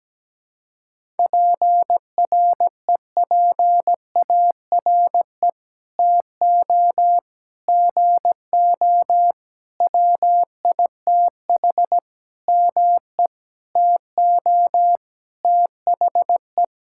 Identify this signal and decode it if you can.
Morse code, heard as a keyed tone, about 17 wpm, 700 Hz. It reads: PREPARE TO GO WITH ME TO THE
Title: Sentences from Top 1000 Words - Review Part 6 - 17wpm